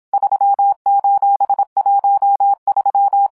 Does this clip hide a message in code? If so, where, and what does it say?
Yes: Morse code from 0.1 to 3.3 s, heard as a keyed tone, about 26 wpm, 800 Hz. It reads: 3:13